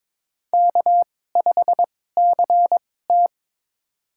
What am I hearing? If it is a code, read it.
Morse code, heard as a keyed tone, about 22 wpm, spaced as for 16 wpm, 700 Hz. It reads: K5CT
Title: USA Call Signs - 4 Characters - 2x Character Spacing - 22wpm